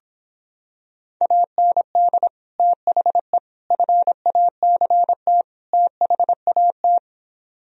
Morse code, heard as a keyed tone, about 26 wpm, 700 Hz. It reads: AND THE FACT THAT